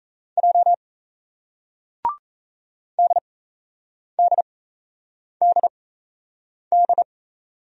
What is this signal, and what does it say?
Morse code, heard as a keyed tone, about 34 wpm, spaced as for 8 wpm, 700 Hz. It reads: J DDDD
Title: Speed-Racing Single Letters 28wpm